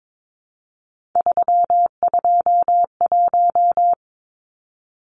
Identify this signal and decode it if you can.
Morse code, heard as a keyed tone, about 22 wpm, 700 Hz. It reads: 321
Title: Numbers - 3-Digit 22wpm